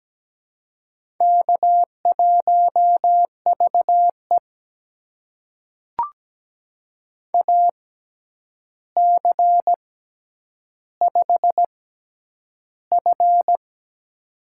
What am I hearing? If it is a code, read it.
Morse code, heard as a keyed tone, about 17 wpm, spaced as for 9 wpm, 700 Hz. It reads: K1VE AC5F
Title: USA Call Signs - 4 Characters - 6x Character Spacing - 17wpm